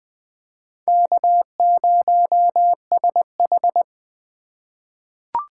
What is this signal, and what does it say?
Morse code, heard as a keyed tone, about 20 wpm, 700 Hz. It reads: K0SH
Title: USA Call Signs - 4 Characters - 4x Character Spacing - 20wpm